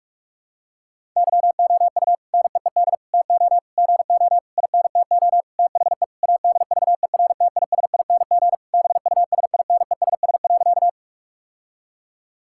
Morse code, heard as a keyed tone, about 46 wpm, 700 Hz. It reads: YOU NEED TO GO INTO THE ADVERTISING BUSINESS.
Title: Sentences from Top 1000 Words - Review Part 6 - 45wpm